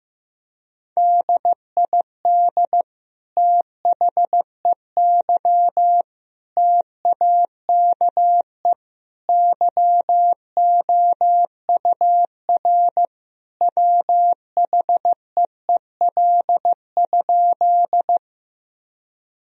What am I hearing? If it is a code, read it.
Morse code, heard as a keyed tone, about 15 wpm, 700 Hz. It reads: DID THEY TAKE YOUR WHEEL?
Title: Sentences from Top 1000 Words - Review Part 3 - 15wpm